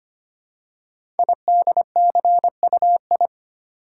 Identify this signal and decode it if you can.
Morse code, heard as a keyed tone, about 25 wpm, 690 Hz. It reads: IDCUI
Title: Code Groups - 5 Letter:Numbers - 25wpm